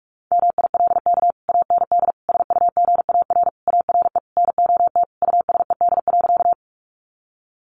Morse code, heard as a keyed tone, about 44 wpm, 700 Hz. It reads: MILK AND SUGAR ARE NOT USED.
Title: Sentences from Top 1500 Words 45wpm